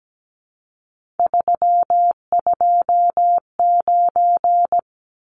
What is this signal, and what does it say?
Morse code, heard as a keyed tone, about 17 wpm, 700 Hz. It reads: 329
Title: Numbers - 3-Digit 17wpm